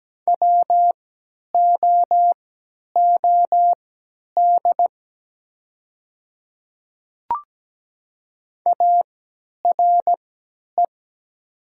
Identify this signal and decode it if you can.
Morse code, heard as a keyed tone, about 17 wpm, spaced as for 10 wpm, 700 Hz. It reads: WOOD ARE
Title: 3x Character Spacing - Top 300 Words - 17wpm